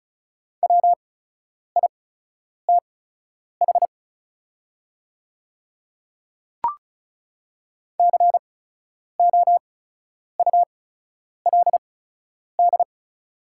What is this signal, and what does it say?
Morse code, heard as a keyed tone, about 34 wpm, spaced as for 10 wpm, 700 Hz. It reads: WITH COULD
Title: 8x Character Spacing - Top 100 Words - 35wpm